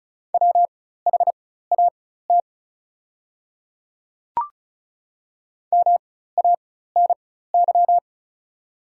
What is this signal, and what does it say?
Morse code, heard as a keyed tone, about 36 wpm, spaced as for 16 wpm, 690 Hz. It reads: WHAT MANY
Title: 4x Character Spacing - Top 100 Words - 35wpm